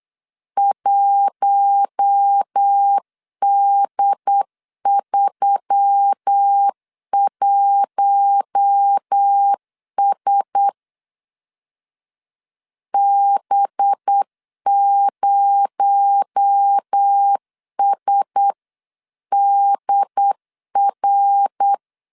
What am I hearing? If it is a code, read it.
Morse code, heard as a keyed tone, about 8 wpm, 800 Hz. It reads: 1D31S B0SDR
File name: 8wpm